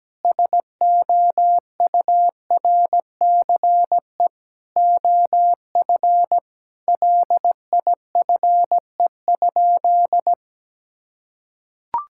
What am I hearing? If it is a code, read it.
Morse code, heard as a keyed tone, about 17 wpm, 700 Hz. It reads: SOURCE OF LIFE?
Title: Sentences from Top 1000 Words - Review Part 5 - 17wpm